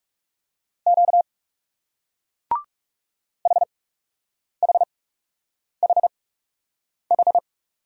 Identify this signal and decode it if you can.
Morse code, heard as a keyed tone, about 39 wpm, spaced as for 8 wpm, 700 Hz. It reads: Q HHHH